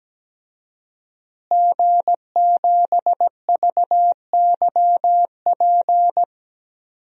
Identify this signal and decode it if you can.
Morse code, heard as a keyed tone, about 17 wpm, 700 Hz. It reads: G7VYP
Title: DXCC-100 Call Signs - 5 Characters - Repeated 2x - 17wpm